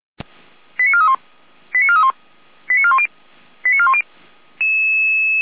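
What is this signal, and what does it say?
Five-tone selective call (ZVEI), heard as quick runs of five tones, tones of about 70 ms. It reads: 89431 89431 89410 89410
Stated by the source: ZVEI five-tone selective call received by SDR.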